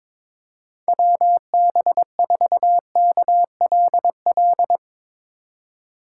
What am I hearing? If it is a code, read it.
Morse code, heard as a keyed tone, about 22 wpm, 700 Hz. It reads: WB4KLL
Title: USA Call Signs - 6 Characters - 8x Character Spacing - 22wpm